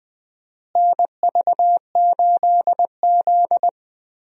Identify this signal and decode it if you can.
Morse code, heard as a keyed tone, about 20 wpm, 700 Hz. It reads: NV8Z
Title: USA Call Signs - 4 Characters - Repeated 3x - 20wpm